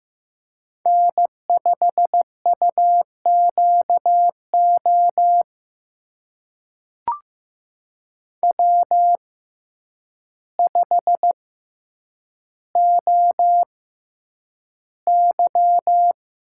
Morse code, heard as a keyed tone, about 15 wpm, spaced as for 8 wpm, 700 Hz. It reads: N5UQO W5OY
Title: USA Call Signs - 5 Characters - 6x Character Spacing - 15wpm